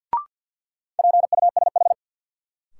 Morse code, heard as a keyed tone, about 47 wpm, 700 Hz. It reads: PUSH
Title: Sentences from Top 1300 Words - New Word Drill Alt - 50wpm